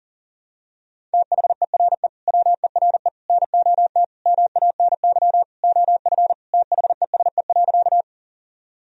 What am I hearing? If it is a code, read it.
Morse code, heard as a keyed tone, about 37 wpm, 700 Hz. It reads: THERE WERE NOT MANY OF THESE.
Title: Sentences from Top 1000 Words - Review Part 1 - 40wpm